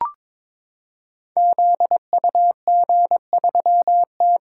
Morse code, heard as a keyed tone, about 22 wpm, 700 Hz. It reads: ZUG3T
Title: Code Groups - 5 Letter:Numbers - 22wpm